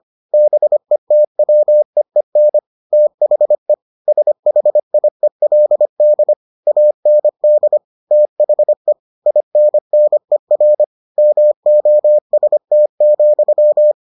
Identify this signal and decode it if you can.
Morse code, heard as a keyed tone, about 25 wpm, 600 Hz. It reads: BETWEEN THE SHIELD AND THE INNER MOST,